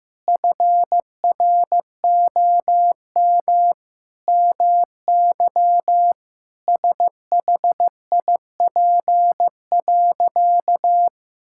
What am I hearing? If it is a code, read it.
Morse code, heard as a keyed tone, about 15 wpm, 700 Hz. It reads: FROM MY SHIP.
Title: Sentences from Top 1000 Words - Review Part 5 - 15wpm